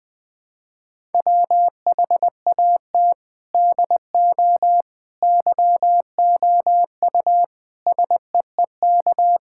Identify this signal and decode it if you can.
Morse code, heard as a keyed tone, about 21 wpm, 700 Hz. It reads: WHAT DO YOU SEEK
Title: Sentences from Top 1600 Words 20wpm